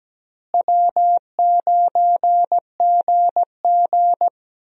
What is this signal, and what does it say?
Morse code, heard as a keyed tone, about 17 wpm, 700 Hz. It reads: W9GG
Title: USA Call Signs - 4 Characters - Repeated 3x - 17wpm